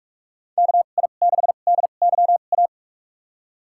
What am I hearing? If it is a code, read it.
Morse code, heard as a keyed tone, about 46 wpm, spaced as for 32 wpm, 700 Hz. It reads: KI6DYA
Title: USA Call Signs - 2x Character Spacing - 45wpm